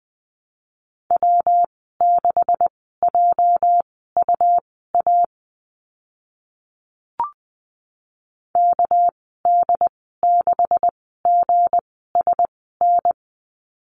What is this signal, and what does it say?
Morse code, heard as a keyed tone, about 20 wpm, spaced as for 14 wpm, 700 Hz. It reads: W6JUA KD6GSN
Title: 213 Lesson 34g - Number 8 - Call Signs 20wpm